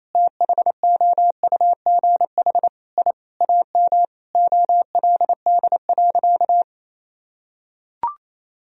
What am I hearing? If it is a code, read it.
Morse code, heard as a keyed tone, about 28 wpm, 700 Hz. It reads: THOUGH I AM OLD.
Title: Sentences from Top 1000 Words - Review Part 3 - 28wpm